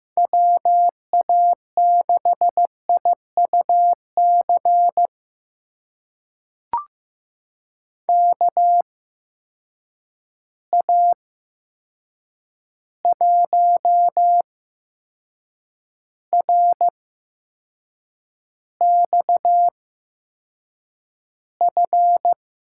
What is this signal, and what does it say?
Morse code, heard as a keyed tone, about 15 wpm, spaced as for 6 wpm, 700 Hz. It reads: WA6IUC KA1RXF
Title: USA Call Signs - 6 Characters - 8x Character Spacing - 15wpm